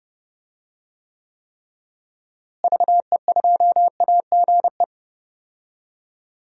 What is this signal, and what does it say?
Morse code, heard as a keyed tone, about 32 wpm, 700 Hz. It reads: VE2AGE
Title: DXCC-100 Call Signs - 6 Characters - Repeated 3x - 30wpm